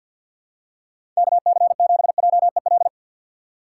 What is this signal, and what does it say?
Morse code, heard as a keyed tone, about 46 wpm, 690 Hz. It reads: KC7JEL
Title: USA Call Signs - 3x Character Spacing - 50wpm